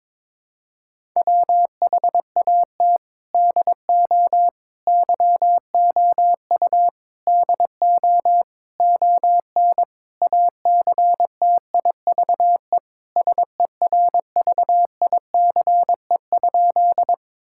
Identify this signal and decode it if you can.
Morse code, heard as a keyed tone, about 22 wpm, 700 Hz. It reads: WHAT DO YOU DO ON ACTIVE SERVICE?
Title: Sentences from Top 1000 Words - Review Part 6 - 22wpm